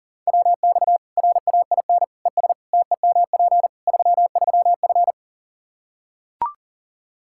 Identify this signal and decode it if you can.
Morse code, heard as a keyed tone, about 40 wpm, 700 Hz. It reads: WX RAIN ES TEMP 33F
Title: Practice QSO - Exchange 2 - Info - 40wpm